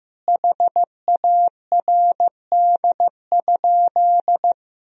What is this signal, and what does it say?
Morse code, heard as a keyed tone, about 15 wpm, 700 Hz. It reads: HARD?
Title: Sentences from Top 1000 Words - Review Part 4 - 15wpm